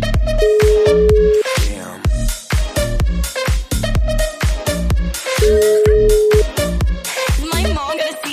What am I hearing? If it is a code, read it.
Ringback tone (Europe, 425 Hz), with noise.